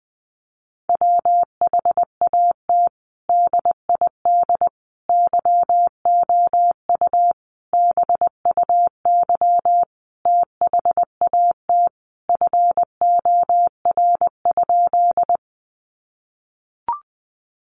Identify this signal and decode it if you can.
Morse code, heard as a keyed tone, about 19 wpm, 700 Hz. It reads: WHAT DID YOU BUY THAT FOR?
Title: Sentences from Top 700 Words 20wpm